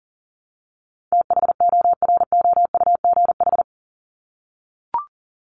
Morse code, heard as a keyed tone, about 40 wpm, 700 Hz. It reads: THOROUGH
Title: LY Suffix - Top 100 Words with Base Words 40wpm